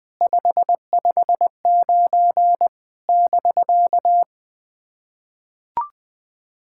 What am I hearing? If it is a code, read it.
Morse code, heard as a keyed tone, about 21 wpm, 700 Hz. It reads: 559 <BK>
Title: SOTA QSO Practice 20wpm